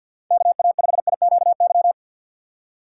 Morse code, heard as a keyed tone, about 46 wpm, 700 Hz. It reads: KA5IQY
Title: USA Call Signs - 6 Characters - 3x Character Spacing - 50wpm